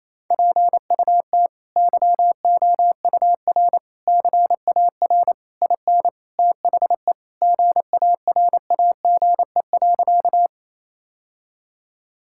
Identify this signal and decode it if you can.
Morse code, heard as a keyed tone, about 28 wpm, 700 Hz. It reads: PUT YOUR CAR IN THE GARAGE.
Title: Sentences from Top 1600 Words 28wpm